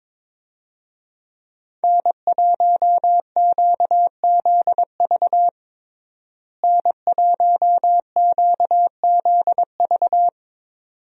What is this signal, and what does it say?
Morse code, heard as a keyed tone, about 22 wpm, 700 Hz. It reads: N1QZV N1QZV
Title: USA Call Signs - 5 Characters - Repeated 2x - 22wpm